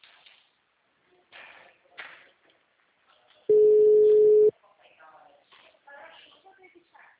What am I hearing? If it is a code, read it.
Ringback tone (Europe, 425 Hz), with speech.
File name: out- -2003- - - .73010